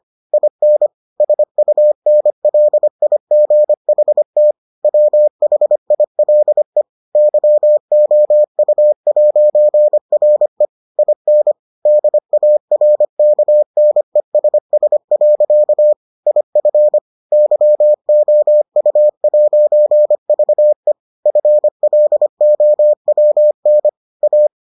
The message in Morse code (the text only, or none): IN SUNLIGHT WHILE YOU'RE IN DARKNESS. IF YOU'VE FLOWN A